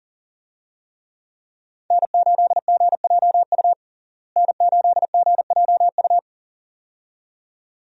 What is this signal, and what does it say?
Morse code, heard as a keyed tone, about 40 wpm, 700 Hz. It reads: N8GJU N8GJU
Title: USA Call Signs - 5 Characters - Repeated 2x - 40wpm